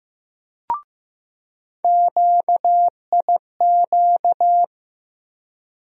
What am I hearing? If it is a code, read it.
Morse code, heard as a keyed tone, about 15 wpm, 700 Hz. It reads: QIQ
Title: USA Call Sign Postfixes - 3 Letters 15wpm